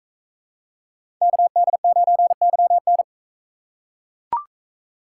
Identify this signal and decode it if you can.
Morse code, heard as a keyed tone, about 40 wpm, 700 Hz. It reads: KD9YN